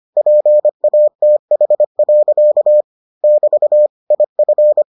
Morse code, heard as a keyed tone, about 25 wpm, 600 Hz. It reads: PATH. <BT> IF